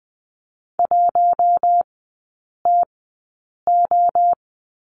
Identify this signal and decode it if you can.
Morse code, heard as a keyed tone, about 20 wpm, spaced as for 8 wpm, 700 Hz. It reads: 1TO
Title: 041 Lesson 8e - Number 1 - Three Characters 20wpm